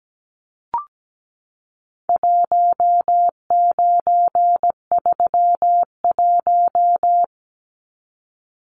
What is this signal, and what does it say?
Morse code, heard as a keyed tone, about 17 wpm, 700 Hz. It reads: 1931
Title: Numbers - 4-Digit 17wpm